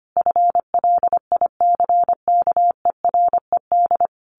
Morse code, heard as a keyed tone, about 25 wpm, 700 Hz. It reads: FLICKERED
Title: A Sound Of Thunder 25wpm